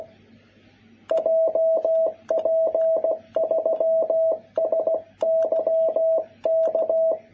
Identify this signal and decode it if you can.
Morse code, heard as a keyed tone, about 16 wpm, 660 Hz. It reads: JP3SYK
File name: jh8cbh / 2021cw